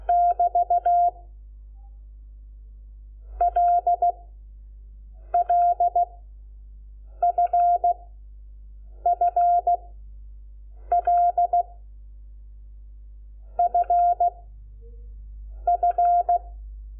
Morse code, heard as a keyed tone, about 16 wpm, 700 Hz. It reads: <BT> L L F F L F F
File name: Lecciones / CLASE Nº5